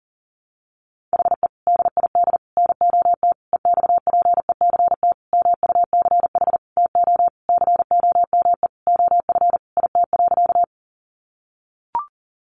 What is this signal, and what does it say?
Morse code, heard as a keyed tone, about 40 wpm, 700 Hz. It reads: HE DID NOT EXPECT MUCH TO COME OF IT.
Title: Sentences from Top 1100 Words 40wpm